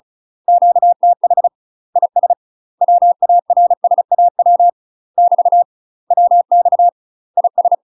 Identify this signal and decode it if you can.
Morse code, heard as a keyed tone, about 34 wpm, 700 Hz. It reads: QTH IS WARSAW <BT> WX IS
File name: UK MASTER QSOs TRAINING / 35WPM / UK MASTER QSOs 0024